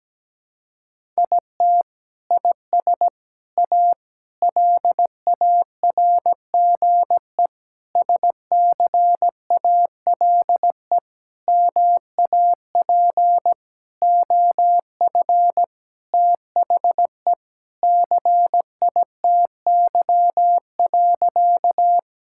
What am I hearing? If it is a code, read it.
Morse code, heard as a keyed tone, about 17 wpm, 700 Hz. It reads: IT IS A LARGE SCALE MAP OF THE CITY.
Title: Sentences from Top 1000 Words - Review Part 5 - 17wpm